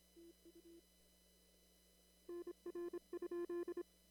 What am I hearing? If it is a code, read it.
Morse code, heard as a keyed tone, about 26 wpm, 340 Hz. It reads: TU NR?
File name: CW / kt5j wpx cw-op-k5pi